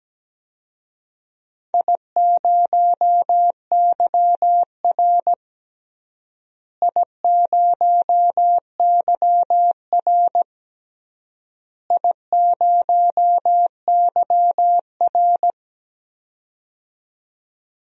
Morse code, heard as a keyed tone, about 17 wpm, 700 Hz. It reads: I0YR I0YR I0YR
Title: DXCC-100 Call Signs - 4 Characters - Repeated 3x - 17wpm